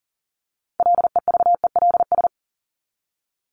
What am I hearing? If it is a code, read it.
Morse code, heard as a keyed tone, about 40 wpm, 700 Hz. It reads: LEVELS
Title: ES:S Suffix - Top 100 Words with Base Words 40wpm